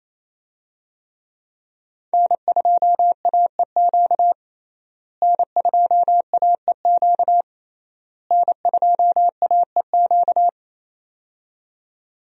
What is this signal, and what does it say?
Morse code, heard as a keyed tone, about 28 wpm, 700 Hz. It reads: N2AEQ N2AEQ N2AEQ
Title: USA Call Signs - 5 Characters - Repeated 3x - 28wpm